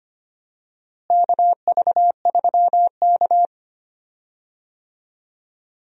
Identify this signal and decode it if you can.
Morse code, heard as a keyed tone, about 25 wpm, 700 Hz. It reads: KV3K
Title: USA Call Signs - 4 Characters - Repeated 2x - 25wpm